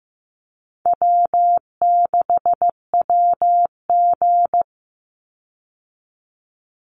Morse code, heard as a keyed tone, about 15 wpm, 700 Hz. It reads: W6WG
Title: Easy Call Signs - 15wpm